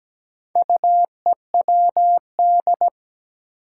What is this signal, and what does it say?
Morse code, heard as a keyed tone, about 17 wpm, 700 Hz. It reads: UEWD
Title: Four Challenging Letter-Number Combinations 17wpm